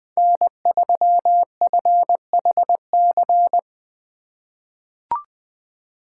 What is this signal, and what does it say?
Morse code, heard as a keyed tone, about 21 wpm, 700 Hz. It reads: N3FHC
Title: USA Call Signs - 5 Characters - 2x Character Spacing - 20wpm